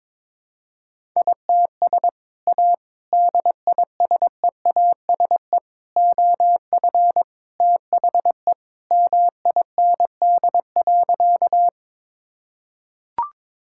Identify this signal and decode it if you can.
Morse code, heard as a keyed tone, about 21 wpm, 700 Hz. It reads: ITS A DISEASE OF THE MIND.
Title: Sentences from Top 1000 Words - Review Part 6 - 22wpm